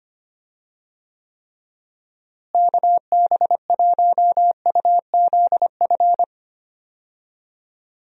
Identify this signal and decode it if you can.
Morse code, heard as a keyed tone, about 25 wpm, 700 Hz. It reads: KB1UZF
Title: USA Call Signs - 6 Characters - Repeated 2x - 25wpm